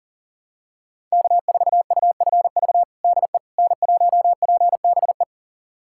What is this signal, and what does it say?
Morse code, heard as a keyed tone, about 40 wpm, 700 Hz. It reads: K4UFV DE N1PBE
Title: USA Callsign DE Callsign - 5 Characters - 40wpm